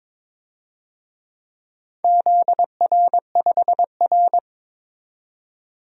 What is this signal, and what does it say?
Morse code, heard as a keyed tone, about 22 wpm, 700 Hz. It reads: ZR5R
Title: DXCC-100 Call Signs - 4 Characters - 22wpm